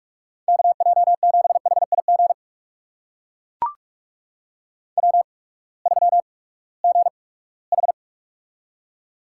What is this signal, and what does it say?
Morse code, heard as a keyed tone, about 44 wpm, spaced as for 22 wpm, 700 Hz. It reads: KJ7HIG W3GH
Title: USA Call Signs - 8x Character Spacing - 45wpm